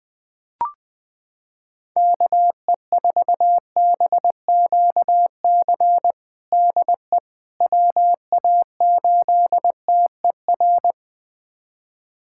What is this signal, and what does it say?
Morse code, heard as a keyed tone, about 20 wpm, 700 Hz. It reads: KE4BQC DE WA8TER
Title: USA Callsign DE Callsign 20wpm